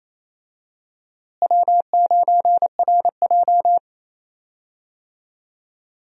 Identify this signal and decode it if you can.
Morse code, heard as a keyed tone, about 28 wpm, 700 Hz. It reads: W9RJ
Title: USA Call Signs - 4 Characters - Repeated 3x - 28wpm